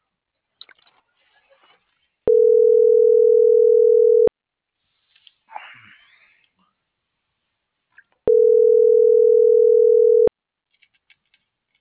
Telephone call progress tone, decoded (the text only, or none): ringback tone